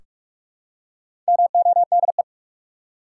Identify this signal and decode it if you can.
Morse code, heard as a keyed tone, about 46 wpm, 700 Hz. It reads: MODE
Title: Four Letter Words - 45wpm